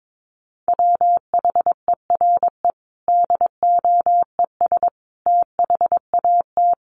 Morse code, heard as a keyed tone, about 22 wpm, 700 Hz. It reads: WHERE DOES THAT
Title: Sentences from Top 1200 Words 22wpm